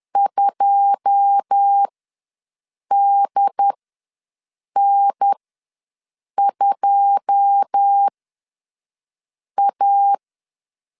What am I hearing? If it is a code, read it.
Morse code, heard as a keyed tone, about 10 wpm, spaced as for 5 wpm, 800 Hz. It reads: 2DN2A
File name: LEZIONI / 9 2 ws 10wpm 4130